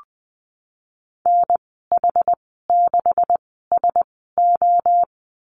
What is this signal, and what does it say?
Morse code, heard as a keyed tone, about 20 wpm, spaced as for 15 wpm, 700 Hz. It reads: NH6SO
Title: 154 Lesson 25g - Letter p - Call Signs 20wpm